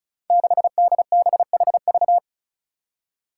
Morse code, heard as a keyed tone, about 36 wpm, 700 Hz. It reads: 6DBHV
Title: Five Challenging Letter-Number Combinations 35wpm